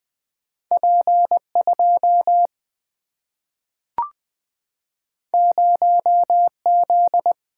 Morse code, heard as a keyed tone, about 20 wpm, 700 Hz. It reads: P2 0Z
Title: Single-Letter Single-Number 20wpm